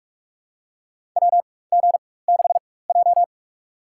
Morse code, heard as a keyed tone, about 46 wpm, spaced as for 21 wpm, 700 Hz. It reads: WG6J
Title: USA Call Signs - 4x Character Spacing - 45wpm